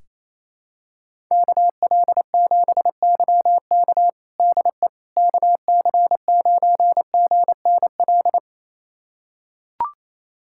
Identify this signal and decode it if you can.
Morse code, heard as a keyed tone, about 28 wpm, 700 Hz. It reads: KL7YK DE KC9GNL
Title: USA Callsign DE Callsign 28wpm